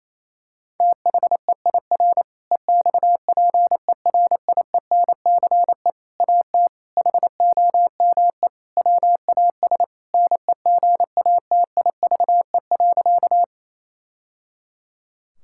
Morse code, heard as a keyed tone, about 28 wpm, 700 Hz. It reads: THEIR EXPERIENCE AT HOME WAS NEGATIVE.
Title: Sentences from Top 1000 Words - Review Part 6 - 28wpm